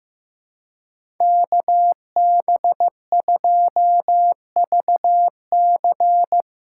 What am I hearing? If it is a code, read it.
Morse code, heard as a keyed tone, about 15 wpm, 700 Hz. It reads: KB2VC
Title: USA Callsign DE Callsign - 5 Characters - 15wpm